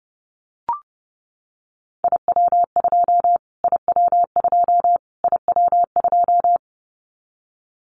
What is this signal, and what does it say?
Morse code, heard as a keyed tone, about 30 wpm, 700 Hz. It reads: IW2 IW2 IW2
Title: Callsign Prefixes for Top 25 DXCC - Repeated 3x - 30wpm